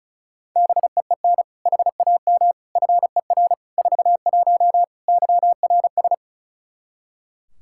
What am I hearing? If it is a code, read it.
Morse code, heard as a keyed tone, about 34 wpm, 700 Hz. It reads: BEEN HAM FER 41 YRS
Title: Practice QSO - Exchange 3 - More Info - 35wpm